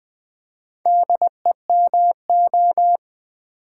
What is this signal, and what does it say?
Morse code, heard as a keyed tone, about 20 wpm, 700 Hz. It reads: DEMO
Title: Build A Word - 4 Letters At A Time - 11-Letter Words - 20wpm